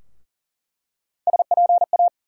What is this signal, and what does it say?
Morse code, heard as a keyed tone, about 40 wpm, 700 Hz. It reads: SPA
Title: USA Call Sign Postfixes - 3 Letters 40wpm